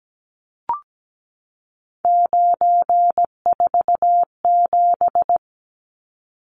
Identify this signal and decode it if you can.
Morse code, heard as a keyed tone, about 17 wpm, 700 Hz. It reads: 947
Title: Numbers - 3-Digit 17wpm